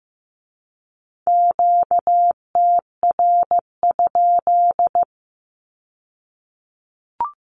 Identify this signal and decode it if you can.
Morse code, heard as a keyed tone, about 15 wpm, 700 Hz. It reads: QTR?
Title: Q-Codes with Spoken Meaning - 15wpm